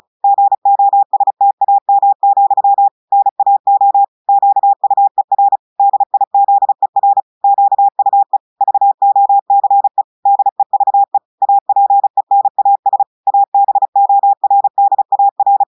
Morse code, heard as a keyed tone, about 35 wpm, 800 Hz. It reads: GOSTAM, NAO QUER DIZER QUE VOCE DEVE APENAS ABORDAR
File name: portugais texte en clair 2 vitesse 35 mots minutes